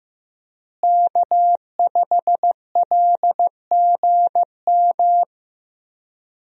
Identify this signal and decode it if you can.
Morse code, heard as a keyed tone, about 15 wpm, 700 Hz. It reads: K5LGM
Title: USA Call Signs - 5 Characters - Repeated 2x - 15wpm